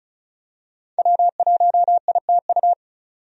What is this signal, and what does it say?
Morse code, heard as a keyed tone, about 36 wpm, 700 Hz. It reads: W1ITU
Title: USA Call Signs - 5 Characters - 35wpm